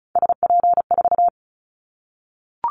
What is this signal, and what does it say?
Morse code, heard as a keyed tone, about 36 wpm, 700 Hz. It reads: SP4
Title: Callsign Prefixes for Top 25 DXCC - 35wpm